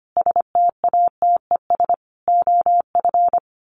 Morse code, heard as a keyed tone, about 25 wpm, 690 Hz. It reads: STATES OF
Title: Sets of 5 Words Encore 25wpm